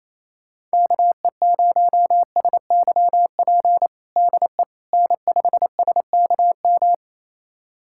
Morse code, heard as a keyed tone, about 28 wpm, 700 Hz. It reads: KE0SYP DE N5SKM
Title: USA Callsign DE Callsign 28wpm